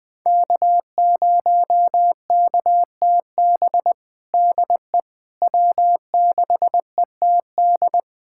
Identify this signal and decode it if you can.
Morse code, heard as a keyed tone, about 21 wpm, 700 Hz. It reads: K0KTB DE W6ETD
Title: USA Callsign DE Callsign - 5 Characters - 20wpm